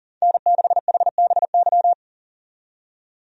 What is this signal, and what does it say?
Morse code, heard as a keyed tone, about 40 wpm, 700 Hz. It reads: N6HBY
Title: USA Call Signs - 5 Characters - 40wpm